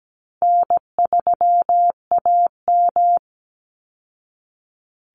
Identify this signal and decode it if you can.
Morse code, heard as a keyed tone, about 17 wpm, 700 Hz. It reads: N3AM
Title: Easy Call Signs Repeated 3x - 17wpm